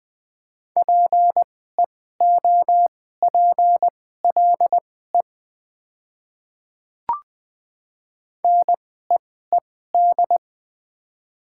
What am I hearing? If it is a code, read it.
Morse code, heard as a keyed tone, about 20 wpm, spaced as for 14 wpm, 700 Hz. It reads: PEOPLE NEED
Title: 2x Character Spacing - Top 300 Words - 20wpm